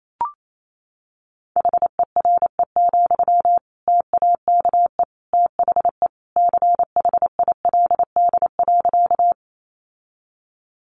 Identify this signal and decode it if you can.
Morse code, heard as a keyed tone, about 28 wpm, 700 Hz. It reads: HERE, TAKE THE CHILD.
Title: Sentences from Top 800 Words 28wpm